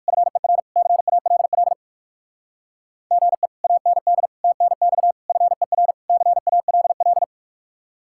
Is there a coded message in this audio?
Yes: Morse code from 0.1 to 7.2 s, heard as a keyed tone, about 57 wpm, 700 Hz. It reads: FER CALL GE AND TNX FER CALL